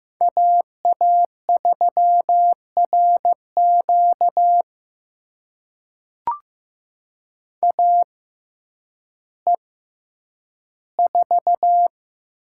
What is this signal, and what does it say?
Morse code, heard as a keyed tone, about 15 wpm, spaced as for 9 wpm, 700 Hz. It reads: AA3RQ AE4
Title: USA Call Signs - 5 Characters - 6x Character Spacing - 15wpm